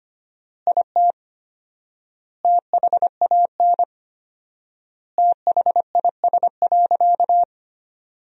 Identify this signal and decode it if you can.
Morse code, heard as a keyed tone, about 25 wpm, 700 Hz. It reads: IT THAN THIS.
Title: 4x Word-Spacing - Sentences from Top 100 Words - 25wpm